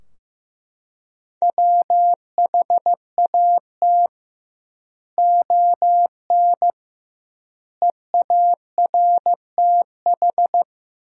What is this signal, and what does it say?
Morse code, heard as a keyed tone, about 15 wpm, 700 Hz. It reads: WHAT ON EARTH
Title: 2x Word-Spacing - Sentences from Top 200 Words - 15wpm